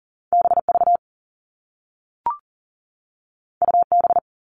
Morse code, heard as a keyed tone, about 42 wpm, 700 Hz. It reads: BV UB
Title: All Cut Numbers - 2-Digit 40wpm